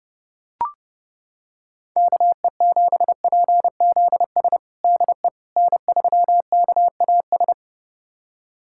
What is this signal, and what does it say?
Morse code, heard as a keyed tone, about 32 wpm, 700 Hz. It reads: KE7PZS DE N3KAS
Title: USA Callsign DE Callsign 30wpm